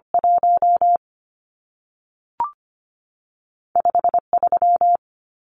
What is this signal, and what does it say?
Morse code, heard as a keyed tone, about 25 wpm, 700 Hz. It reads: 1 5 3
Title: Numbers - 2-Digit 25wpm